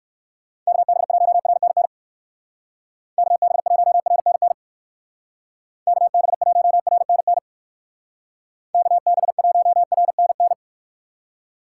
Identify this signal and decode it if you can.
Morse code, heard as a keyed tone, about 68 wpm, 700 Hz. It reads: KB1RNN KB1RNN KB1RNN KB1RNN